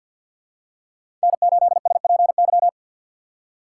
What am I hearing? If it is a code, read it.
Morse code, heard as a keyed tone, about 51 wpm, 700 Hz. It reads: N8SPY
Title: USA Call Signs - 5 Characters - 4x Character Spacing - 50wpm